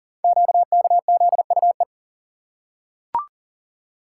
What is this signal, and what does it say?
Morse code, heard as a keyed tone, about 38 wpm, 700 Hz. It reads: QKZUE
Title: Code Groups - 5 Letters - 40wpm